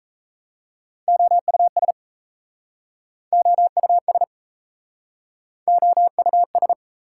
Morse code, heard as a keyed tone, about 38 wpm, 690 Hz. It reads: OUS OUS OUS